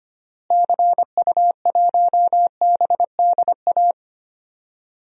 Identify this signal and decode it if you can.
Morse code, heard as a keyed tone, about 25 wpm, 700 Hz. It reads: CU1BDA
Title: DXCC-100 Call Signs - 6 Characters - Repeated 2x - 25wpm